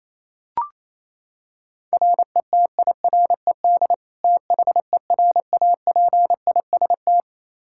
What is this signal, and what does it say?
Morse code, heard as a keyed tone, about 28 wpm, 700 Hz. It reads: RETIRED THERAPIST
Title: Practice QSO - Exchange 3 - More Info - 28wpm